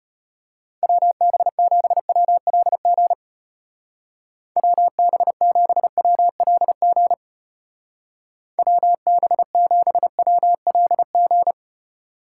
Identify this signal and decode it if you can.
Morse code, heard as a keyed tone, about 31 wpm, 700 Hz. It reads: WB7WLG WB7WLG WB7WLG
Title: Speed Racing - USA Call Signs 25wpm